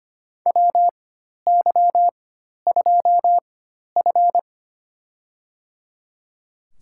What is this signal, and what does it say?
Morse code, heard as a keyed tone, about 25 wpm, spaced as for 12 wpm, 700 Hz. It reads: WY2F
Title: USA Call Signs - 4 Characters - 4x Character Spacing - 25wpm